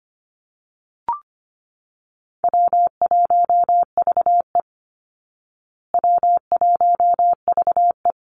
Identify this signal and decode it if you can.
Morse code, heard as a keyed tone, about 25 wpm, 700 Hz. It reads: W1VE W1VE
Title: Easy Call Signs Repeated 2x - 25wpm